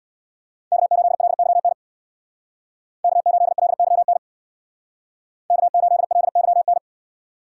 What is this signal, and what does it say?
Morse code, heard as a keyed tone, about 64 wpm, 700 Hz. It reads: K8LYN K8LYN K8LYN